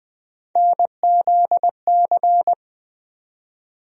Morse code, heard as a keyed tone, about 20 wpm, 700 Hz. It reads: NZC
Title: USA Call Sign Postfixes - 3 Letters 20wpm